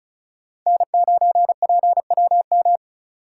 Morse code, heard as a keyed tone, about 36 wpm, 700 Hz. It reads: N9PWM
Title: USA Call Signs - 5 Characters - 2x Character Spacing - 35wpm